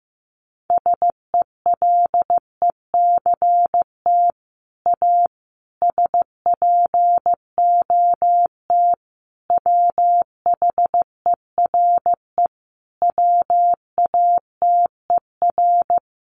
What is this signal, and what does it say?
Morse code, heard as a keyed tone, about 15 wpm, 700 Hz. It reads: SELECT A SPOT WHERE WATER
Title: Sentences from Top 1300 Words 15wpm